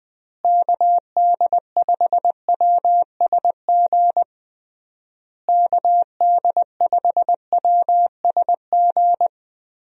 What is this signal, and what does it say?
Morse code, heard as a keyed tone, about 21 wpm, 700 Hz. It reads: KD5WSG KD5WSG
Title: USA Call Signs - 6 Characters - Repeated 3x - 20wpm